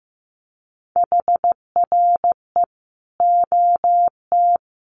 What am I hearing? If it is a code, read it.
Morse code, heard as a keyed tone, about 15 wpm, 700 Hz. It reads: HRE OT
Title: Common QSO Phrases 15wpm